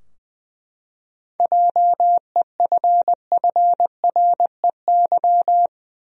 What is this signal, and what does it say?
Morse code, heard as a keyed tone, about 20 wpm, 700 Hz. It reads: JEFFREY
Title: USA Common Names - 20wpm